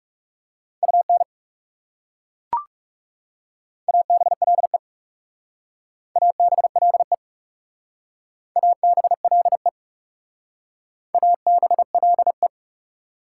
Morse code, heard as a keyed tone, about 38 wpm, 690 Hz. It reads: UN ABLE ABLE ABLE ABLE